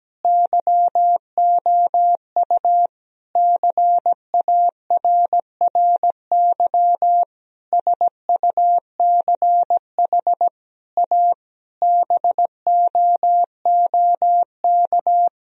Morse code, heard as a keyed tone, about 17 wpm, 700 Hz. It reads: YOU CARRY SUCH A BOOK
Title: Sentences from Top 1000 Words - Review Part 1 - 17wpm